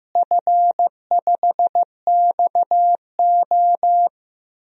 Morse code, heard as a keyed tone, about 16 wpm, 700 Hz. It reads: F5XO
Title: DXCC-100 Call Signs - 4 Characters - Repeated 2x - 15wpm